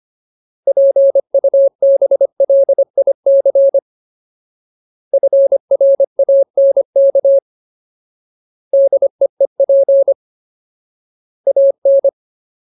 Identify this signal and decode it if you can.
Morse code, heard as a keyed tone, about 25 wpm, 550 Hz. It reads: PUBLIC FRANK DEEP AN
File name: Ultimate Word List / 25WPM Ultimate